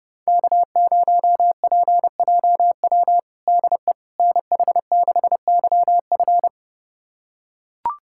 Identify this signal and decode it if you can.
Morse code, heard as a keyed tone, about 32 wpm, 700 Hz. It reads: K0PJW DE NH6YF
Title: USA Callsign DE Callsign - 5 Characters - 30wpm